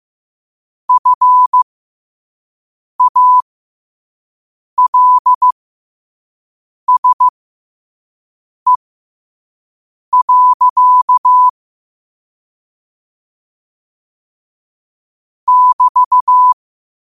Morse code, heard as a keyed tone, about 15 wpm, spaced as for 5 wpm, 1000 Hz. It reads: FALSE. <BT>
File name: qotdfiles / 5wpm